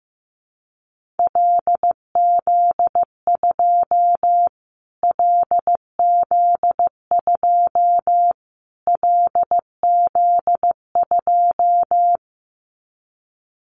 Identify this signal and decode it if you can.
Morse code, heard as a keyed tone, about 15 wpm, 700 Hz. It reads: LZ2 LZ2 LZ2
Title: Callsign Prefixes for Top 25 DXCC - Repeated 3x - 15wpm